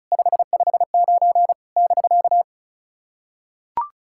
Morse code, heard as a keyed tone, about 36 wpm, 700 Hz. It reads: 559 <BK>
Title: SOTA QSO Practice 35wpm